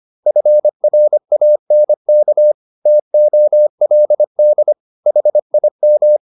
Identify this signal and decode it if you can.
Morse code, heard as a keyed tone, about 25 wpm, 600 Hz. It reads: FRANK TOLD HIM